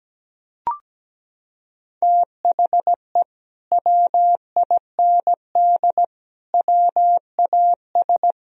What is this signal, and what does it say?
Morse code, heard as a keyed tone, about 17 wpm, 700 Hz. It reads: THE WIND WAS
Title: Sentences from Top 1000 Words - Review Part 2 - 17wpm